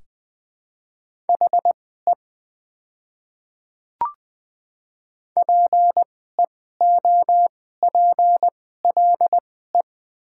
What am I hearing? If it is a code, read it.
Morse code, heard as a keyed tone, about 20 wpm, spaced as for 14 wpm, 700 Hz. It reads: HE PEOPLE
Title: 2x Character Spacing - Top 100 Words - 20wpm